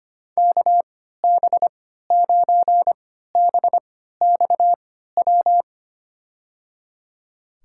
Morse code, heard as a keyed tone, about 25 wpm, spaced as for 14 wpm, 700 Hz. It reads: KB9BXW
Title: USA Call Signs - 3x Character Spacing - 25wpm